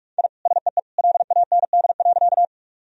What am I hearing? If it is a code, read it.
Morse code, heard as a keyed tone, about 46 wpm, 700 Hz. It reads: I SEE LAND.
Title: Sentences from Top 1000 Words - Review Part 1 - 45wpm